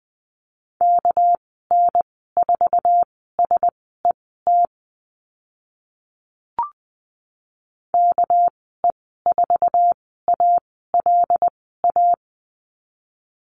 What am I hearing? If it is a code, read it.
Morse code, heard as a keyed tone, about 20 wpm, spaced as for 14 wpm, 700 Hz. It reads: KN4SET KE4ALA
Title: 200 Lesson 32g - Letter k - Call Signs 20wpm